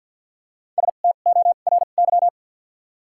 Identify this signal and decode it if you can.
Morse code, heard as a keyed tone, about 53 wpm, spaced as for 37 wpm, 700 Hz. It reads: STORY
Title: 2x Character Spacing - Top 300 Words - 50wpm